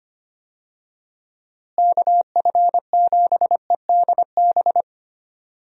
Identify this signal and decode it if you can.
Morse code, heard as a keyed tone, about 25 wpm, 700 Hz. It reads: KF7EDB
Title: USA Call Signs - 6 Characters - Repeated 2x - 25wpm